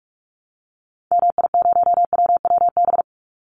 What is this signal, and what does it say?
Morse code, heard as a keyed tone, about 44 wpm, 700 Hz. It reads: MI0WWB
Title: Call Signs Repeated 3x - 45wpm